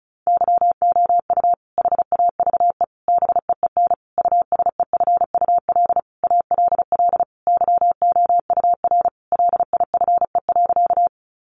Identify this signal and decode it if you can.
Morse code, heard as a keyed tone, about 34 wpm, 700 Hz. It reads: YOU HAVE BEEN USEFUL ALL YOUR LIFE.
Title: Sentences from Top 700 Words 35wpm